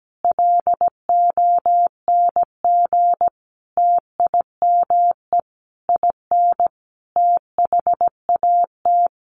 Morse code, heard as a keyed tone, about 17 wpm, 700 Hz. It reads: LONG TIME IN THAT
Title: Sentences from Top 900 Words 17wpm